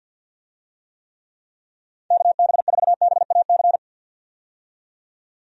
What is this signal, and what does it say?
Morse code, heard as a keyed tone, about 48 wpm, 700 Hz. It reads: KB4BAC
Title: USA Call Signs - 6 Characters - 50wpm